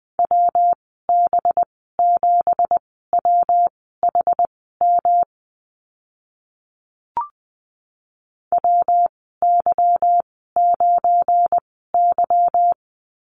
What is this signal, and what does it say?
Morse code, heard as a keyed tone, about 20 wpm, spaced as for 15 wpm, 700 Hz. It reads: WB7WHM WY9Y
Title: 193 Lesson 31g - Letter v - Call Signs 20wpm